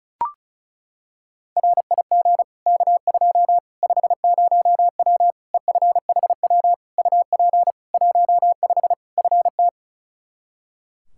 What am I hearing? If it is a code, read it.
Morse code, heard as a keyed tone, about 36 wpm, 700 Hz. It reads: RIG K2 50W EFHW UP 15 FT
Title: Practice QSO - Exchange 2 - Info - 35wpm